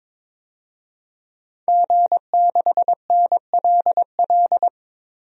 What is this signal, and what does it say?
Morse code, heard as a keyed tone, about 22 wpm, 700 Hz. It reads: G6NLL
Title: DXCC-100 Call Signs - 5 Characters - 22wpm